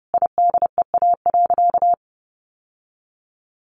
Morse code, heard as a keyed tone, about 30 wpm, 700 Hz. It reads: IDEA.
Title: Sentences from Top 600 Words 30wpm